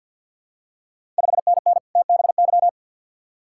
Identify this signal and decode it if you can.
Morse code, heard as a keyed tone, about 52 wpm, 700 Hz. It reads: 5NN TBY
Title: Ontario QSO Party Exchange 50wpm